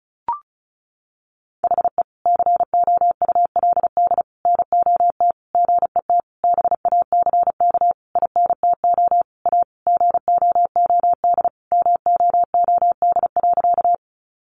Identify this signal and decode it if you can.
Morse code, heard as a keyed tone, about 34 wpm, 700 Hz. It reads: HE COULD NOT GET BACK INTO A GOOD MOOD.
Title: Sentences from Top 1100 Words 35wpm